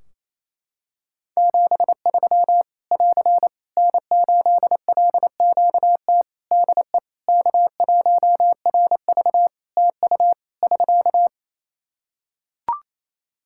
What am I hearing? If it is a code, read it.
Morse code, heard as a keyed tone, about 28 wpm, 700 Hz. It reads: 73 <AR> N8LQT DE K1RV TU <SK>
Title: K1USN SST QSO Practice 28wpm